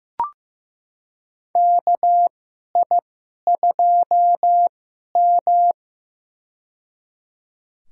Morse code, heard as a keyed tone, about 15 wpm, spaced as for 11 wpm, 700 Hz. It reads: KI2M
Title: USA Call Signs - 4 Characters - 2x Character Spacing - 15wpm